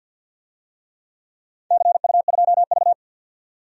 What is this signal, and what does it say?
Morse code, heard as a keyed tone, about 50 wpm, 700 Hz. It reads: CU2V
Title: DXCC-100 Call Signs - 4 Characters - 50wpm